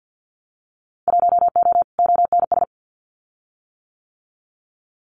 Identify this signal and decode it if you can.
Morse code, heard as a keyed tone, about 50 wpm, 700 Hz. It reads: 1O ONS
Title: Winter Field Day Exchanges 50wpm